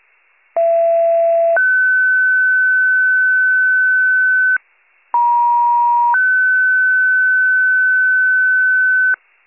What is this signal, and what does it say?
Two-tone page, heard as a one-second tone, then a three-second tone, about 670 then 1500 Hz, about 950 then 1500 Hz. It is heard from 0.6 to 9.1 s.